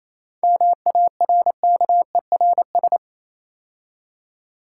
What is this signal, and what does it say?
Morse code, heard as a keyed tone, about 27 wpm, 690 Hz. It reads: MARKERS
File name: Easy Words - Speed-Racing with Hints - 22wpm